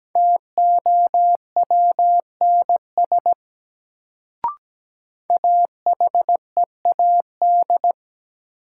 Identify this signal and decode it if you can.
Morse code, heard as a keyed tone, about 17 wpm, 700 Hz. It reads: TOWNS AHEAD
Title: Sentences from Top 1000 Words - Review Part 6 - New Word Drill - 17wpm